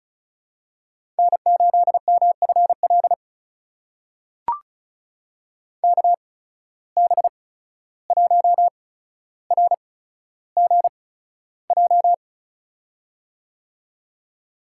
Morse code, heard as a keyed tone, about 34 wpm, spaced as for 14 wpm, 700 Hz. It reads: N8MFL KB1RGJ
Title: USA Call Signs - 8x Character Spacing - 35wpm